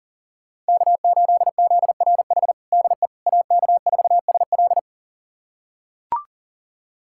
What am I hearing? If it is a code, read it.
Morse code, heard as a keyed tone, about 40 wpm, 700 Hz. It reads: K8ZRH DE AK4SL
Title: USA Callsign DE Callsign - 5 Characters - 40wpm